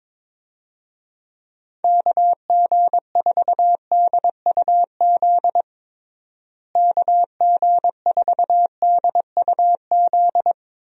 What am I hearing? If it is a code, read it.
Morse code, heard as a keyed tone, about 22 wpm, 700 Hz. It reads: KG4DUZ KG4DUZ
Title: USA Call Signs - 6 Characters - Repeated 3x - 22wpm